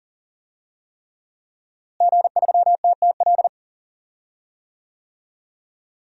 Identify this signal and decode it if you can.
Morse code, heard as a keyed tone, about 40 wpm, 700 Hz. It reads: G3TTL
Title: DXCC-100 Call Signs - 5 Characters - Repeated 3x - 40wpm